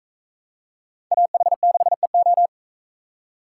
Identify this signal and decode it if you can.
Morse code, heard as a keyed tone, about 44 wpm, 700 Hz. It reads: AH6EO